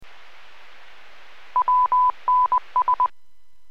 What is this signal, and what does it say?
Morse code, heard as a keyed tone, about 20 wpm, 1000 Hz. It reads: WNS